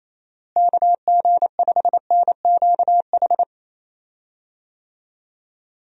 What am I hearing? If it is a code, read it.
Morse code, heard as a keyed tone, about 28 wpm, 700 Hz. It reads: KG5NQH
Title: USA Call Signs - 6 Characters - Repeated 2x - 28wpm